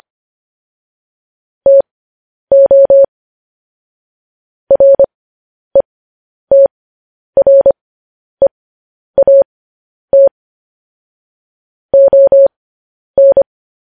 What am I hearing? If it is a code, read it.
Morse code, heard as a keyed tone, about 25 wpm, spaced as for 10 wpm, 550 Hz. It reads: TO RETREAT ON
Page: news headlines in morse code at 10 wpm